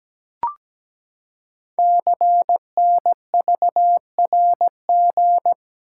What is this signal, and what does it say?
Morse code, heard as a keyed tone, about 17 wpm, 700 Hz. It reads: CNVRG
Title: Code Groups - 5 Letters - 17wpm